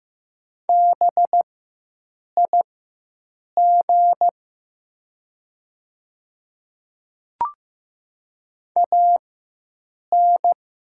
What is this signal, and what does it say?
Morse code, heard as a keyed tone, about 15 wpm, spaced as for 7 wpm, 700 Hz. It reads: BIG AN
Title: 4x Character Spacing - Top 200 Words - 15wpm